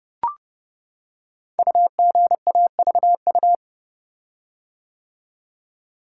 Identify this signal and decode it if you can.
Morse code, heard as a keyed tone, about 30 wpm, 700 Hz. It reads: UGAVU
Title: Code Groups - 5 Letter:Numbers - 30wpm